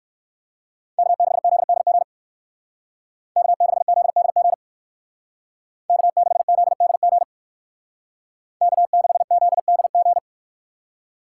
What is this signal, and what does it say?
Morse code, heard as a keyed tone, about 53 wpm, 700 Hz. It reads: K6ZDG K6ZDG K6ZDG K6ZDG
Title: Speed Racing - USA 1x3 Call Signs 45wpm